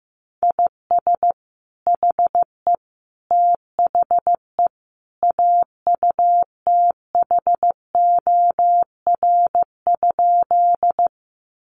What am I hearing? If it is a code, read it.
Morse code, heard as a keyed tone, about 15 wpm, 700 Hz. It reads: IS HE THE AUTHOR?
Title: Sentences from Top 1200 Words 15wpm